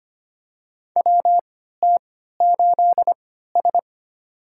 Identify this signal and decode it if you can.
Morse code, heard as a keyed tone, about 25 wpm, spaced as for 14 wpm, 700 Hz. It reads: WT8S
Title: USA Call Signs - 4 Characters - 3x Character Spacing - 25wpm